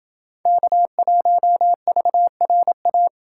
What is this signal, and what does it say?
Morse code, heard as a keyed tone, about 29 wpm, 700 Hz. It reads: K1VRA